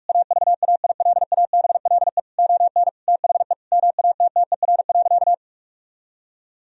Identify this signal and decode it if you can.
Morse code, heard as a keyed tone, about 46 wpm, 700 Hz. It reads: AVAILABLE ON THE MATTER.
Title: Sentences from Top 1000 Words - Review Part 3 - 45wpm